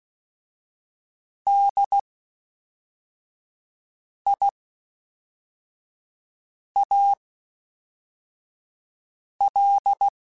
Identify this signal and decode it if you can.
Morse code, heard as a keyed tone, about 16 wpm, spaced as for 4 wpm, 800 Hz. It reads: DIAL
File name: Special INGs / Special INGs-mp3s / Fast ing @10